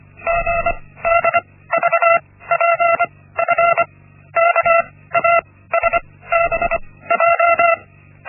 Morse code, heard as a keyed tone, about 24 wpm, spaced as for 16 wpm, 680 Hz. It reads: GDVPF KASBJ
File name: Linksmakalnis-1